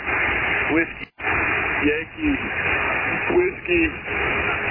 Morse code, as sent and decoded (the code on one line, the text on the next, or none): none